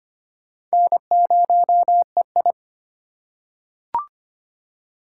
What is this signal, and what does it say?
Morse code, heard as a keyed tone, about 25 wpm, 700 Hz. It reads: N0EI
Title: USA Call Signs - 4 Characters - 4x Character Spacing - 25wpm